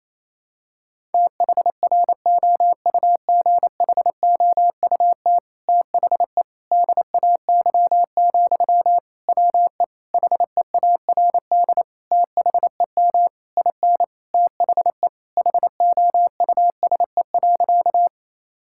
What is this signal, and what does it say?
Morse code, heard as a keyed tone, about 27 wpm, 700 Hz. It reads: THROUGHOUT THE DAY, WE HEARD THEM IN THE HOUSE.
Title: Sentences from Top 1000 Words - Review Part 5 - 28wpm